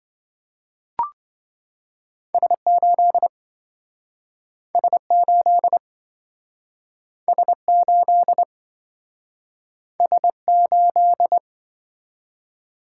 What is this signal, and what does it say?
Morse code, heard as a keyed tone, about 27 wpm, 700 Hz. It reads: S8 S8 S8 S8